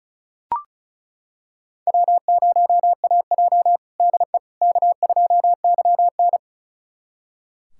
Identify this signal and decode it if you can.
Morse code, heard as a keyed tone, about 36 wpm, 700 Hz. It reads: W0AJ DE K2YN
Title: USA Callsign DE Callsign - 4 Characters - 35wpm